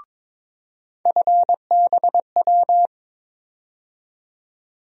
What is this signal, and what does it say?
Morse code, heard as a keyed tone, about 22 wpm, 700 Hz. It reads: FBW
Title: USA Call Sign Postfixes - 3 Letters 22wpm